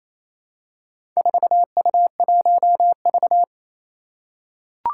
Morse code, heard as a keyed tone, about 27 wpm, 700 Hz. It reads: 4U1V
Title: Four Challenging Letter-Number Combinations 28wpm